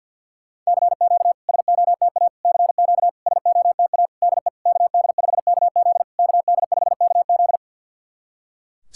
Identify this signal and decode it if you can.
Morse code, heard as a keyed tone, about 51 wpm, 690 Hz. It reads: CQ SOTA CQ SOTA DE KD5KZ KD5KZ
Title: SOTA QSO Practice 50wpm